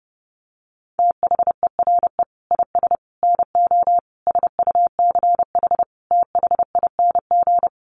Morse code, heard as a keyed tone, about 30 wpm, 700 Hz. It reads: THERE IS NO SUCH THING
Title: Sets of 5 Words Encore 30wpm